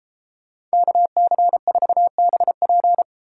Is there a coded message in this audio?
Yes: Morse code from 0.7 to 3.0 s, heard as a keyed tone, about 31 wpm, 700 Hz. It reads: KC4BP